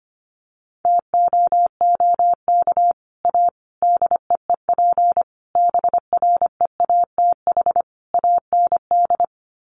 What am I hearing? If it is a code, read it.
Morse code, heard as a keyed tone, about 25 wpm, 700 Hz. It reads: TOOK A DEEP BREATH AND
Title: Sets of 5 Words Encore 25wpm